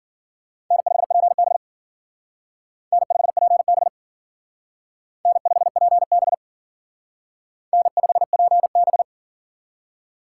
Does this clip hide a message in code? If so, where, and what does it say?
Yes: Morse code from 0.7 to 9.0 s, heard as a keyed tone, about 51 wpm, 700 Hz. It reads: N5PB N5PB N5PB N5PB